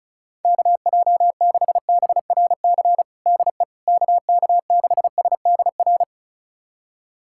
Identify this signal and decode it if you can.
Morse code, heard as a keyed tone, about 36 wpm, 700 Hz. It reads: KJ6BRC DE KK6SDR
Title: USA Callsign DE Callsign - 6 Characters - 35wpm